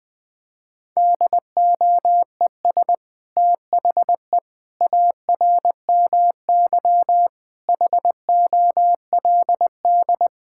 Morse code, heard as a keyed tone, about 20 wpm, 700 Hz. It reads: DOES THE ARMY HOLD
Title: Sentences from Top 1000 Words - Review Part 6 - 20wpm